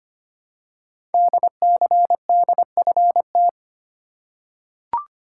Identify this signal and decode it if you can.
Morse code, heard as a keyed tone, about 25 wpm, 700 Hz. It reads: DCDFT
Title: Code Groups - 5 Letters - 25wpm